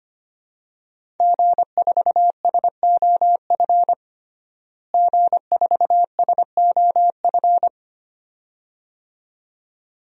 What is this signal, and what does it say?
Morse code, heard as a keyed tone, about 25 wpm, 700 Hz. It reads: G4SOF G4SOF
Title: DXCC-100 Call Signs - 5 Characters - Repeated 2x - 25wpm